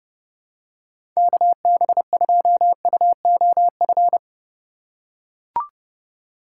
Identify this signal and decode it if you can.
Morse code, heard as a keyed tone, about 29 wpm, 700 Hz. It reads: KB2UOF